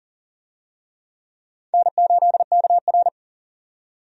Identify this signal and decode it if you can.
Morse code, heard as a keyed tone, about 40 wpm, 700 Hz. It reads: N8KR
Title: USA Call Signs - 4 Characters - 40wpm